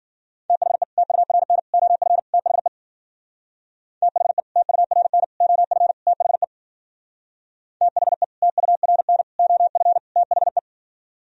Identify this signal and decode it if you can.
Morse code, heard as a keyed tone, about 49 wpm, 700 Hz. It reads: THE TURN OF THE THE TURN OF THE THE TURN OF THE
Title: Speed-Racing - Sets of 4 Words 40wpm